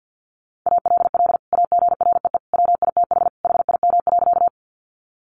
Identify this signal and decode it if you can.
Morse code, heard as a keyed tone, about 50 wpm, 700 Hz. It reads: ALL AGREE WITH HIM.
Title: Sentences from Top 1100 Words 50wpm